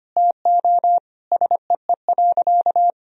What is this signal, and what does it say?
Morse code, heard as a keyed tone, about 25 wpm, 700 Hz. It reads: TO SEE.
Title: Sentences from Top 1000 Words - Review Part 3 - 25wpm